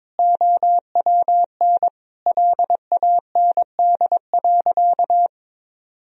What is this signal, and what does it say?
Morse code, heard as a keyed tone, about 22 wpm, 700 Hz. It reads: OWN LAND.
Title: Sentences from Top 1000 Words - Review Part 2 - 22wpm